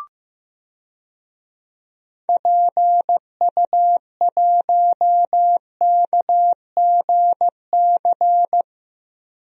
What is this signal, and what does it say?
Morse code, heard as a keyed tone, about 15 wpm, 700 Hz. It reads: PU1KGC
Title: DXCC-100 Call Signs - 6 Characters - Repeated 3x - 15wpm